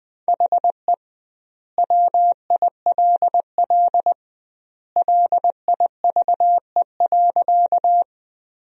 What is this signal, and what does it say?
Morse code, heard as a keyed tone, about 20 wpm, 700 Hz. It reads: HE WILL LIVE.
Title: 2x Word-Spacing - Sentences from Top 200 Words - 20wpm